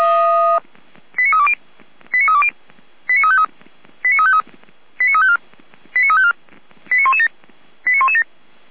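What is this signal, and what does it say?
Five-tone selective call (ZVEI), heard as quick runs of five tones, tones of about 70 ms. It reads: 89320 89320 89353 89353 89354 89354 89117 89117
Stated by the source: ZVEI five-tone call 89167 received with an SDR, recorded on the Feuerwehr Orlamünde server.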